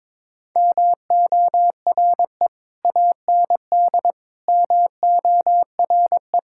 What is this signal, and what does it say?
Morse code, heard as a keyed tone, about 22 wpm, 700 Hz. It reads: MORE AND MORE
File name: practice / 22wpm / Sentences with 3-Word Phrases - 22wpm